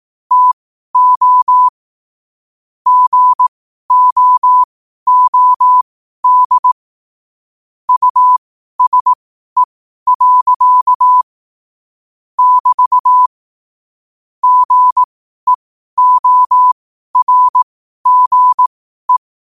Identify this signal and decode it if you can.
Morse code, heard as a keyed tone, about 18 wpm, spaced as for 12 wpm, 1000 Hz. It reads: TO GOOD USE. <BT> GEORGE
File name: qotdfiles / 12wpm